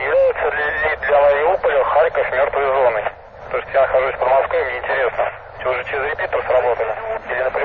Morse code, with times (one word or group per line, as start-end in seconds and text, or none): none